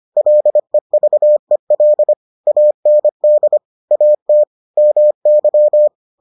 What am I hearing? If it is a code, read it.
Morse code, heard as a keyed tone, about 25 wpm, 600 Hz. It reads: LEVEL AND AT MY